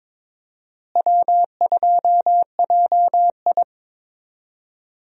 Morse code, heard as a keyed tone, about 23 wpm, 700 Hz. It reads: W2JI
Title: USA Call Signs - 4 Characters - Repeated 2x - 22wpm